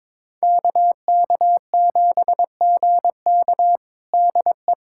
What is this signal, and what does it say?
Morse code, heard as a keyed tone, about 22 wpm, 700 Hz. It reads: KK7GK DE
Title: USA Callsign DE Callsign - 5 Characters - 22wpm